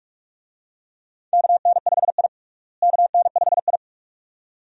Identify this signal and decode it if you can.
Morse code, heard as a keyed tone, about 42 wpm, 700 Hz. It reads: KN5I KN5I
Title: USA Call Signs - 4 Characters - Repeated 2x - 45wpm